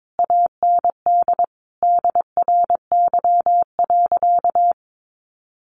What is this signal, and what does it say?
Morse code, heard as a keyed tone, about 22 wpm, 700 Hz. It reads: AND DRY.
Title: Sentences from Top 1500 Words 22wpm